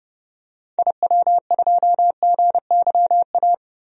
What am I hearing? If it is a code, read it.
Morse code, heard as a keyed tone, about 30 wpm, 700 Hz. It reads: IW2GYA
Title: DXCC-100 Call Signs - 6 Characters - Repeated 2x - 30wpm